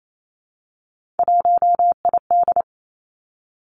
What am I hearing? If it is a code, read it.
Morse code, heard as a keyed tone, about 28 wpm, 700 Hz. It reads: 1ID
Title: Three Challenging Letter-Number Combinations 28wpm